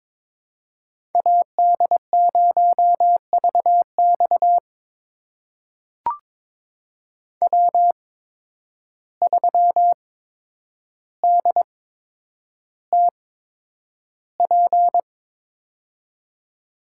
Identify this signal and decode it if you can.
Morse code, heard as a keyed tone, about 21 wpm, spaced as for 9 wpm, 700 Hz. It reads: AD0VX W3DTP
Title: USA Call Signs - 5 Characters - 8x Character Spacing - 22wpm